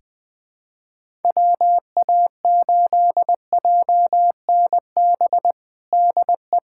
Morse code, heard as a keyed tone, about 20 wpm, 700 Hz. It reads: WA8JNB DE
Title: USA Callsign DE Callsign - 6 Characters - 20wpm